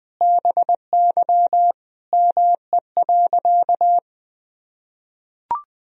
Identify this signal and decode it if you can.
Morse code, heard as a keyed tone, about 20 wpm, 700 Hz. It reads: BY ME.
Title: Sentences from Top 500 Words and Abbreviations 20wpm